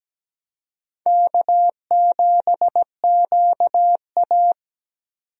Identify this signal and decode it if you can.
Morse code, heard as a keyed tone, about 17 wpm, 700 Hz. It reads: K7QA
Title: USA Call Signs - 4 Characters - 17wpm